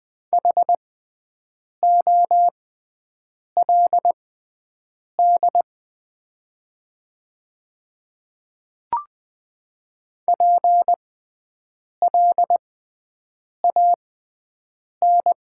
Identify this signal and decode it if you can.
Morse code, heard as a keyed tone, about 20 wpm, spaced as for 7 wpm, 700 Hz. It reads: HOLD PLAN
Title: 6x Character Spacing - Top 500 Words - 20wpm